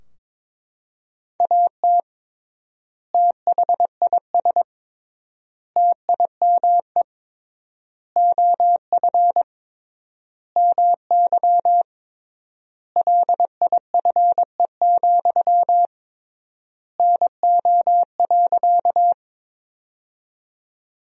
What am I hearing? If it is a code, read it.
Morse code, heard as a keyed tone, about 22 wpm, 700 Hz. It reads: AT THIS TIME OF MY LIFE, NO.
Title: 3x Word-Spacing - Sentences from Top 300 Words - 22wpm